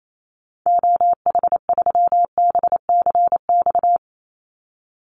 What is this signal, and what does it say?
Morse code, heard as a keyed tone, about 28 wpm, 700 Hz. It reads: OH3BCX
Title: Call Signs - 28wpm